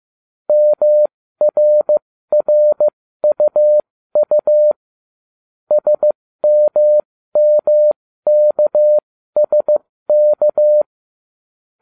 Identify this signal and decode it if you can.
Morse code, heard as a keyed tone, about 15 wpm, spaced as for 13 wpm, 600 Hz. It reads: MRRUU SMMKSK